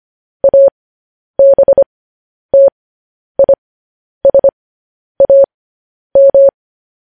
Morse code, heard as a keyed tone, about 25 wpm, spaced as for 10 wpm, 550 Hz. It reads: ABTISAM
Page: news headlines in morse code at 10 wpm